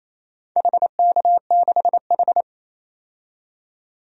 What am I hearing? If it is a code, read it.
Morse code, heard as a keyed tone, about 27 wpm, 700 Hz. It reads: HK6H
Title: DXCC-100 Call Signs - 4 Characters - 28wpm